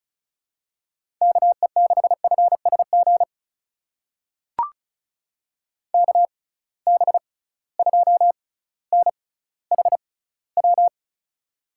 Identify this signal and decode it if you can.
Morse code, heard as a keyed tone, about 36 wpm, spaced as for 18 wpm, 700 Hz. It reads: KE6FSG KB2NHW
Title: USA Call Signs - 6 Characters - 6x Character Spacing - 35wpm